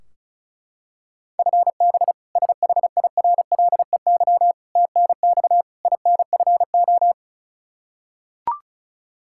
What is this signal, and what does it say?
Morse code, heard as a keyed tone, about 36 wpm, 700 Hz. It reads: FB SHIRLEY TNX INFO
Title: K1USN SST QSO Practice 35wpm